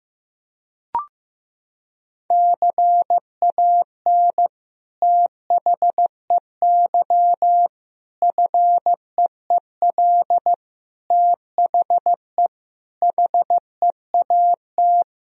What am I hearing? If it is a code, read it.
Morse code, heard as a keyed tone, about 16 wpm, 700 Hz. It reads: CAN THEY FEEL THE HEAT
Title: Sentences from Top 1000 Words - Review Part 3 - 15wpm